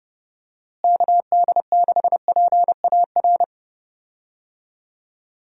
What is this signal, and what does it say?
Morse code, heard as a keyed tone, about 30 wpm, 700 Hz. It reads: KD6PAR
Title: USA Call Signs - 6 Characters - Repeated 3x - 30wpm